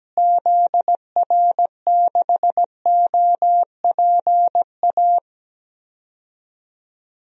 Morse code, heard as a keyed tone, about 17 wpm, 700 Hz. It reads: ZR6OPA
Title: DXCC-100 Call Signs - 6 Characters - Repeated 2x - 17wpm